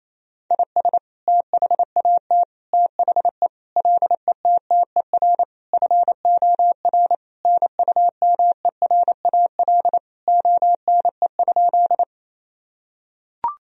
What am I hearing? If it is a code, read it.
Morse code, heard as a keyed tone, about 27 wpm, 700 Hz. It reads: IS THAT THE LETTER FOR NUMERAL ONE?
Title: Sentences from Top 1000 Words - Review Part 2 - 28wpm